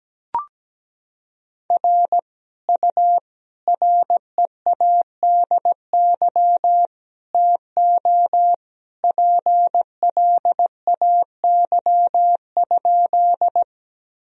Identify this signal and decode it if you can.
Morse code, heard as a keyed tone, about 17 wpm, 700 Hz. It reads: R U READY TO PLAY?
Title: Sentences from Top 400 Words and Abbreviations 17wpm